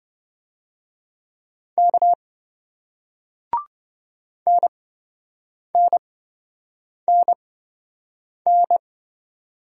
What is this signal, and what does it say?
Morse code, heard as a keyed tone, about 27 wpm, spaced as for 7 wpm, 700 Hz. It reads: K NNNN